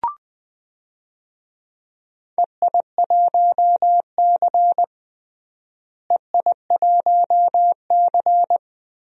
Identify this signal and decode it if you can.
Morse code, heard as a keyed tone, about 20 wpm, 700 Hz. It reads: EI1C EI1C
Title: DXCC-100 Call Signs - 4 Characters - Repeated 2x - 20wpm